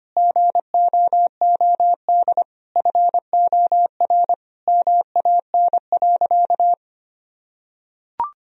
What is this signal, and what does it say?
Morse code, heard as a keyed tone, about 25 wpm, 700 Hz. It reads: GOOD FOR MAN.
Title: Sentences from Top 1000 Words - Review Part 3 - 25wpm